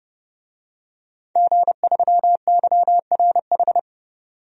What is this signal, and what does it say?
Morse code, heard as a keyed tone, about 30 wpm, 700 Hz. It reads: G3YRH
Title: DXCC-100 Call Signs - 5 Characters - 30wpm